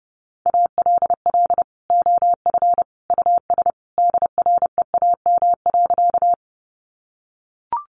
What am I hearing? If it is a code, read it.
Morse code, heard as a keyed tone, about 30 wpm, 700 Hz. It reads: ALL OF US DREAM.
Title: Sentences from Top 1500 Words 30wpm